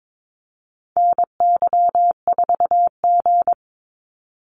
Morse code, heard as a keyed tone, about 22 wpm, 700 Hz. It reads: NY4G
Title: NA SOTA Activator Callsigns 22wpm